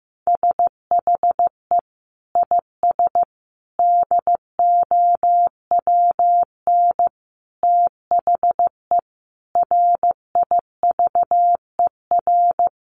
Morse code, heard as a keyed tone, about 15 wpm, 700 Hz. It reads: SHE IS DOWN THE RIVER
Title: Sentences from Top 1400 Words 15wpm